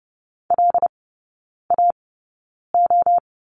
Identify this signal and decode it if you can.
Morse code, heard as a keyed tone, about 30 wpm, spaced as for 9 wpm, 700 Hz. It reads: LAO
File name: Lesson 18e - Letter m - Three Characters 30wpm